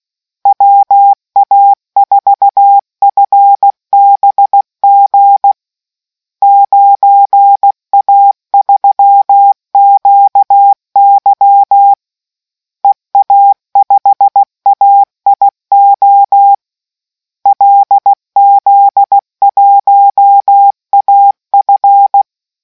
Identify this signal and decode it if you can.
Morse code, heard as a keyed tone, about 16 wpm, 800 Hz. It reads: WA4FBG 9A3QY EA5AIO LZ1AF